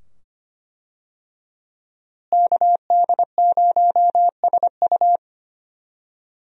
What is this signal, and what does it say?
Morse code, heard as a keyed tone, about 25 wpm, 700 Hz. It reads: KD0SU
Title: USA Call Signs - 5 Characters - Repeated 2x - 25wpm